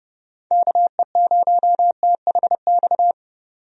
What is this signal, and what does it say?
Morse code, heard as a keyed tone, about 30 wpm, 700 Hz. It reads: KE0THX
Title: USA Call Signs - 6 Characters - 30wpm